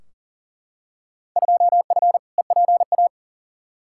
Morse code, heard as a keyed tone, about 40 wpm, 700 Hz. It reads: 2F EPA
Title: Field Day Exchanges - 40wpm